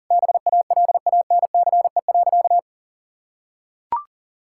Morse code, heard as a keyed tone, about 37 wpm, 700 Hz. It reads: BALANCE.
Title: Sentences from Top 1000 Words - Review Part 6 - 40wpm